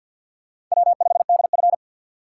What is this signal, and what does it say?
Morse code, heard as a keyed tone, about 53 wpm, 700 Hz. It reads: W5DF
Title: USA Call Signs - 4 Characters - 6x Character Spacing - 50wpm